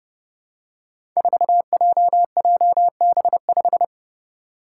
Morse code, heard as a keyed tone, about 29 wpm, 700 Hz. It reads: 4JJB5
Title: Code Groups - 5 Letter:Numbers - 30wpm